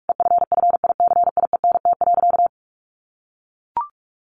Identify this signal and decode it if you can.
Morse code, heard as a keyed tone, about 44 wpm, 700 Hz. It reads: EFFICIENT.
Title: Sentences from Top 1300 Words 45wpm